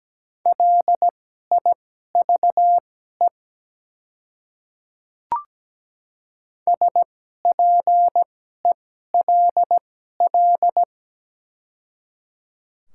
Morse code, heard as a keyed tone, about 17 wpm, spaced as for 12 wpm, 700 Hz. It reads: LIVE SPELL
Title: 2x Character Spacing - Top 200 Words - 17wpm